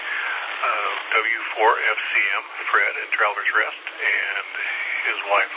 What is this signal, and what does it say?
Emergency Alert System (SAME) tones, heard as harsh buzzing bursts, with speech. It is heard from 1.0 to 4.7 s.